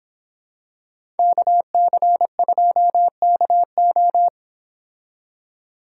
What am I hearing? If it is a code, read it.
Morse code, heard as a keyed tone, about 25 wpm, 700 Hz. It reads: KC2KO